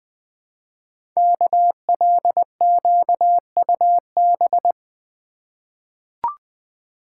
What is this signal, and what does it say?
Morse code, heard as a keyed tone, about 20 wpm, 700 Hz. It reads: KLQUB
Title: Code Groups - 5 Letter:Numbers - 20wpm